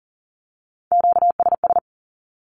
Morse code, heard as a keyed tone, about 38 wpm, 700 Hz. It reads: QSS
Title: Q-Codes with Spoken Meaning - 40wpm